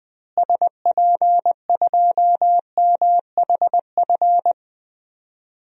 Morse code, heard as a keyed tone, about 21 wpm, 700 Hz. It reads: SP2MHF
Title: DXCC-100 Call Signs - 6 Characters - Repeated 2x - 20wpm